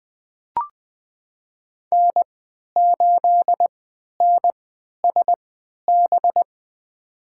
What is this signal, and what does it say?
Morse code, heard as a keyed tone, about 21 wpm, spaced as for 12 wpm, 700 Hz. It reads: N8NSB
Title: USA Call Signs - 5 Characters - 3x Character Spacing - 20wpm